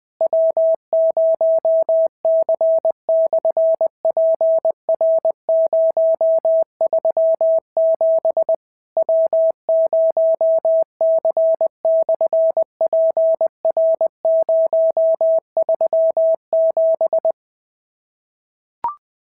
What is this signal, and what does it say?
Morse code, heard as a keyed tone, about 20 wpm, 690 Hz. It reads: W0C/PR037 W0C/PR037
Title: SOTA QSO Practice 20wpm